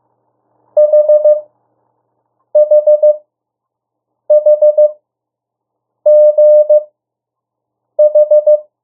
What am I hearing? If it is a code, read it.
Morse code, heard as a keyed tone, about 15 wpm, spaced as for 6 wpm, 600 Hz. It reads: HHHGH
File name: urok 1